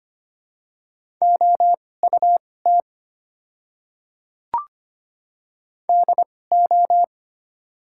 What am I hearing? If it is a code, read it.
Morse code, heard as a keyed tone, about 25 wpm, spaced as for 18 wpm, 700 Hz. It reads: OUT DO
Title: 2x Character Spacing - Top 100 Words - 25wpm